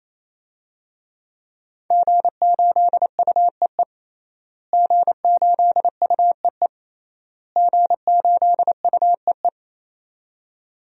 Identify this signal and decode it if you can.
Morse code, heard as a keyed tone, about 28 wpm, 700 Hz. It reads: G8UEE G8UEE G8UEE
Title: DXCC-100 Call Signs - 5 Characters - Repeated 3x - 28wpm